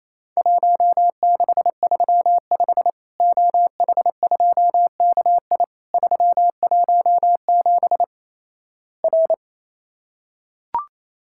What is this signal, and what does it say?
Morse code, heard as a keyed tone, about 28 wpm, 700 Hz. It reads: 1635 OH2KI 317 R
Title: Worked All Europe DX Contest - QTC Exchange - 28wpm